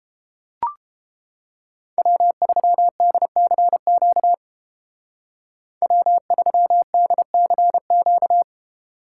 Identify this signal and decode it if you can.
Morse code, heard as a keyed tone, about 31 wpm, 700 Hz. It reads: W3DCQ W3DCQ